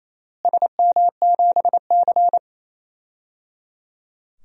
Morse code, heard as a keyed tone, about 28 wpm, 700 Hz. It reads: SM7C
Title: DXCC-100 Call Signs - 4 Characters - Repeated 3x - 28wpm